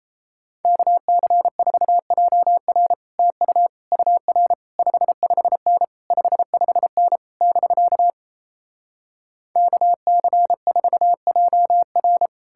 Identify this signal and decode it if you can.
Morse code, heard as a keyed tone, about 37 wpm, 700 Hz. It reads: KC4JR TU UR 55N 55N <BK> KC4JR